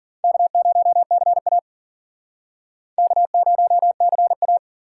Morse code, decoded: K0CA K0CA

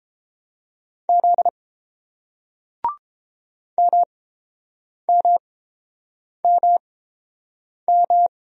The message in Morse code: Z MMMM

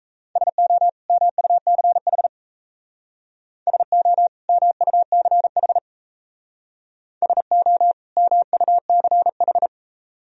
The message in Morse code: SO MUCH SO MUCH SO MUCH

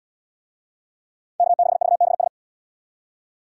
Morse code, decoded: K6VBS

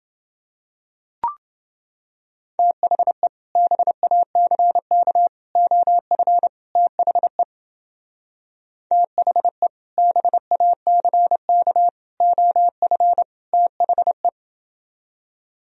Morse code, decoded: THE BACK OF THE THE BACK OF THE

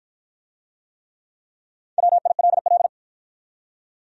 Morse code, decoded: WILL

